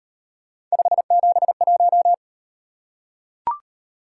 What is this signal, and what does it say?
Morse code, heard as a keyed tone, about 36 wpm, 700 Hz. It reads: 571